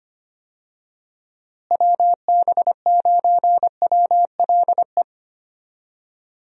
Morse code, heard as a keyed tone, about 25 wpm, 700 Hz. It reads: WB9WLE
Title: USA Call Signs - 6 Characters - Repeated 3x - 25wpm